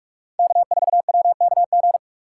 Morse code, heard as a keyed tone, about 46 wpm, 700 Hz. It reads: K4WKG
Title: USA Call Signs - 5 Characters - 6x Character Spacing - 45wpm